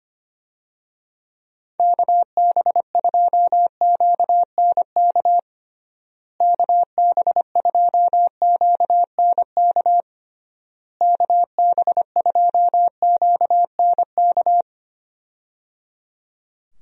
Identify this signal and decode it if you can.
Morse code, heard as a keyed tone, about 25 wpm, 700 Hz. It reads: KB2QNK KB2QNK KB2QNK
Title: USA Call Signs - 6 Characters - Repeated 3x - 25wpm